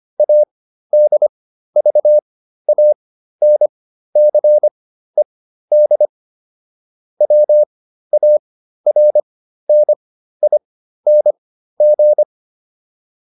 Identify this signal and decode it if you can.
Morse code, heard as a keyed tone, about 25 wpm, spaced as for 13 wpm, 600 Hz. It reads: ADVANCED WARNING